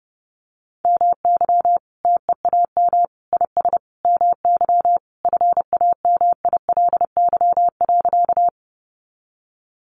Morse code, heard as a keyed tone, about 30 wpm, 700 Hz. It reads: MY TEAM IS MY FAMILY.
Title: Sentences from Top 1500 Words 30wpm